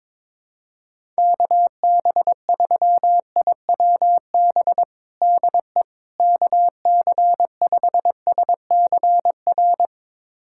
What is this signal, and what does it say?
Morse code, heard as a keyed tone, about 22 wpm, 700 Hz. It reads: KB3IWB DE KC5SCR
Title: USA Callsign DE Callsign - 6 Characters - 22wpm